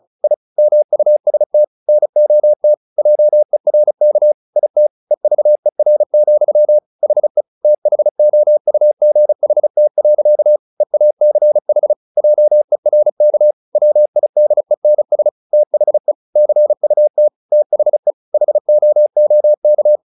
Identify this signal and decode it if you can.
Morse code, heard as a keyed tone, about 34 wpm, 600 Hz. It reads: I MUST NOT JERK IT EVER, HE THOUGHT. EACH JERK WIDENS THE CUT THE HOOK
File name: the old man and the sea / the old man and the sea 1 35wpm